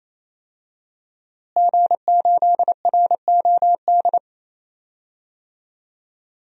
Morse code, heard as a keyed tone, about 27 wpm, 700 Hz. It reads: G8ROD
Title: DXCC-100 Call Signs - 5 Characters - Repeated 3x - 28wpm